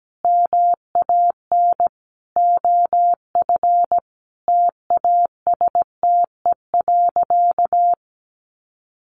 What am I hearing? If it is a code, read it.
Morse code, heard as a keyed tone, about 17 wpm, 700 Hz. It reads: MAN OF TASTE.
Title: Sentences from Top 1100 Words 17wpm